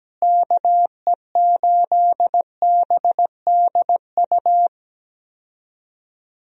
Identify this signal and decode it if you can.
Morse code, heard as a keyed tone, about 17 wpm, 700 Hz. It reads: KE8BDU
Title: USA Call Signs - 6 Characters - Repeated 3x - 17wpm